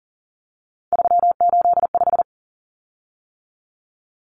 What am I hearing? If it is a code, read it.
Morse code, heard as a keyed tone, about 40 wpm, 700 Hz. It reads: 385
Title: Common Cut Numbers - 3-Digit 40wpm